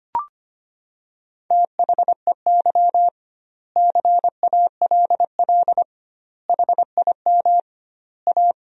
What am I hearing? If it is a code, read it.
Morse code, heard as a keyed tone, about 25 wpm, 700 Hz. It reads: THEY CALL HIM A
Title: 2x Word-Spacing - Sentences from Top 500 Words - 25wpm